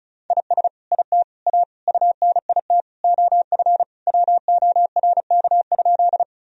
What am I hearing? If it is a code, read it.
Morse code, heard as a keyed tone, about 36 wpm, 700 Hz. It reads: IS IT A UNIT OF WORK?
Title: Sentences from Top 1000 Words - Review Part 2 - 35wpm